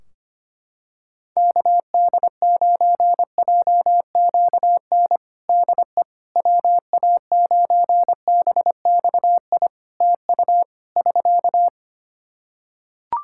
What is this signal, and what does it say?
Morse code, heard as a keyed tone, about 25 wpm, 700 Hz. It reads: KD9JQN DE WA9BXI TU <SK>
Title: Practice QSO - Exchange 4 - Ending - 25wpm